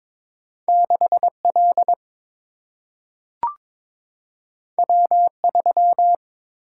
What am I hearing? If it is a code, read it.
Morse code, heard as a keyed tone, about 22 wpm, 700 Hz. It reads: 6 L W 3
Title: Single-Letter Single-Number 22wpm